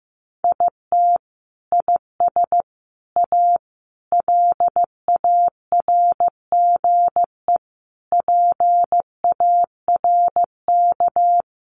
Morse code, heard as a keyed tone, about 15 wpm, 700 Hz. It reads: IT IS A LARGE PARK
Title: Sentences from Top 1300 Words 15wpm